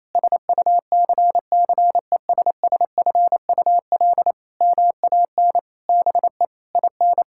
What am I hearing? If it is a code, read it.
Morse code, heard as a keyed tone, about 28 wpm, 700 Hz. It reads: SUCCESSFUL MAN BE IN
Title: Sentences from Top 1000 Words - Review Part 6 - 28wpm